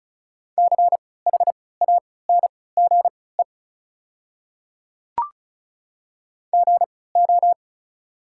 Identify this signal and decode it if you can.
Morse code, heard as a keyed tone, about 34 wpm, spaced as for 20 wpm, 690 Hz. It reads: CHANGE GO
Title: 3x Character Spacing - Top 500 Words - 35wpm